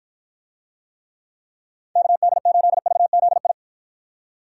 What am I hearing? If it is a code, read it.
Morse code, heard as a keyed tone, about 54 wpm, 700 Hz. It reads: KD8VZI